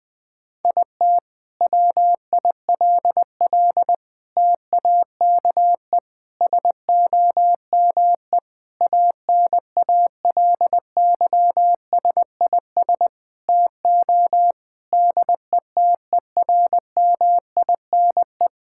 Morse code, heard as a keyed tone, about 20 wpm, 700 Hz. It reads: IT WILL TAKE SOME ANALYSIS TO DETERMINE
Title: Sentences from Top 1000 Words - Review Part 5 - 20wpm